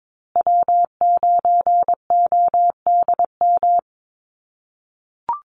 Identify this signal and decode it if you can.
Morse code, heard as a keyed tone, about 22 wpm, 700 Hz. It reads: W9ODM
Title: USA Call Signs Repeated 3x 22wpm